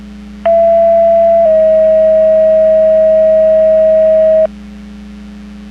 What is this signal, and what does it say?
Two-tone page, heard as a one-second tone, then a three-second tone, about 670 then 640 Hz. It is heard from 0.5 to 4.5 s.
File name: Portage / 08179 18-06 Portage 154.385 Paging